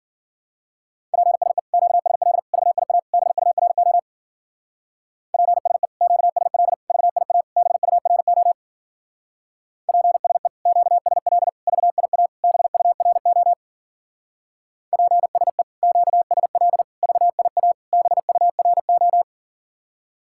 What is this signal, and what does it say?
Morse code, heard as a keyed tone, about 47 wpm, 700 Hz. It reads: PSE QSL VIA BURO PSE QSL VIA BURO PSE QSL VIA BURO PSE QSL VIA BURO
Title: Speed-Racing Common QSO Phrases Encore - 40wpm